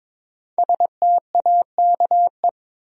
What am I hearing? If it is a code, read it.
Morse code, heard as a keyed tone, about 22 wpm, 700 Hz. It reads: STAKE
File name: 5 Letter Words - First 3 Letters Given - 22wpm